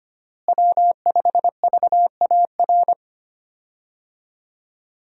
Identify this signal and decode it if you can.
Morse code, heard as a keyed tone, about 25 wpm, 700 Hz. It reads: W5VAR
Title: USA Call Signs - 5 Characters - Repeated 3x - 25wpm